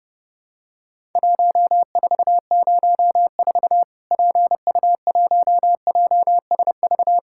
Morse code, heard as a keyed tone, about 30 wpm, 700 Hz. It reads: 1404 PU1JSV
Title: Worked All Europe DX Contest - QTC Exchange - 30wpm